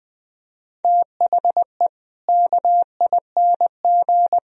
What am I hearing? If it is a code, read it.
Morse code, heard as a keyed tone, about 20 wpm, 700 Hz. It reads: THE KING
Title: Sentences from Top 1000 Words - Review Part 2 - 20wpm